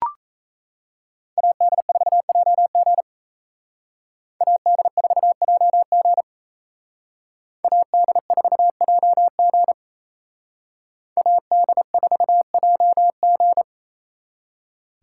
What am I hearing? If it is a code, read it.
Morse code, heard as a keyed tone, about 36 wpm, 700 Hz. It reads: AD4JG AD4JG AD4JG AD4JG